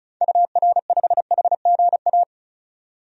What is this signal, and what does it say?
Morse code, heard as a keyed tone, about 34 wpm, 700 Hz. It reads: UR5HGA
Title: DXCC-100 Call Signs - 6 Characters - 35wpm